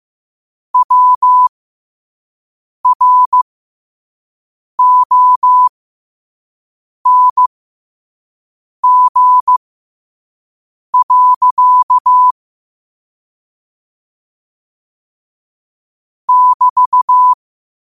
Morse code, heard as a keyed tone, about 15 wpm, spaced as for 5 wpm, 1000 Hz. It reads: WRONG. <BT>
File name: qotdfiles / 5wpm